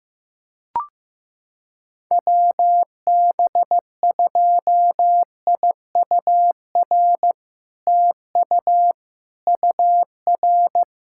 Morse code, heard as a keyed tone, about 16 wpm, 700 Hz. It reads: WB2IUR TU UR
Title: SOTA QSO Practice 15wpm